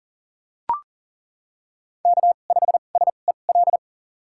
Morse code, heard as a keyed tone, about 40 wpm, spaced as for 29 wpm, 700 Hz. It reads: K5SEL
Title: USA Call Signs - 2x Character Spacing - 40wpm